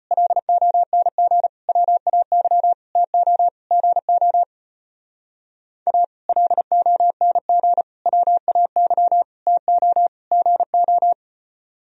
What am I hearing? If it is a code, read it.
Morse code, heard as a keyed tone, about 29 wpm, 700 Hz. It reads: LONG WAY TO GO A LONG WAY TO GO
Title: Speed-Racing - Sets of 5 Words 25wpm